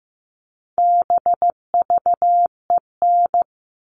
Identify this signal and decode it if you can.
Morse code, heard as a keyed tone, about 15 wpm, 700 Hz. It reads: BVEN
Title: All Cut Numbers - 4-Digit 15wpm